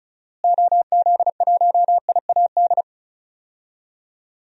Morse code, heard as a keyed tone, about 34 wpm, 700 Hz. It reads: OZ1IAD
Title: DXCC-100 Call Signs - 6 Characters - Repeated 2x - 35wpm